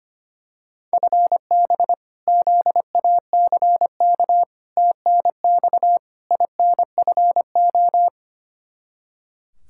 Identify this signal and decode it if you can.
Morse code, heard as a keyed tone, about 25 wpm, 700 Hz. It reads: FB ZACK TNX INFO
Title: Practice QSO - Exchange 3 - More Info - 25wpm